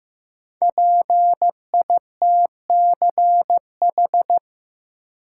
Morse code, heard as a keyed tone, about 15 wpm, 700 Hz. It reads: PITCH
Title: Sentences from Top 1300 Words - New Word Drill Alt - 15wpm